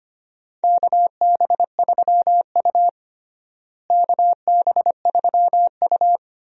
Morse code, heard as a keyed tone, about 25 wpm, 700 Hz. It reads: KB3U KB3U
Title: USA Call Signs - 4 Characters - Repeated 2x - 25wpm